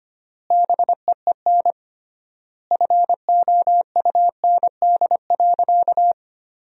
Morse code, heard as a keyed tone, about 25 wpm, 700 Hz. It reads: BEEN FOUND.
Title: 3x Word-Spacing - Sentences from Top 300 Words - 25wpm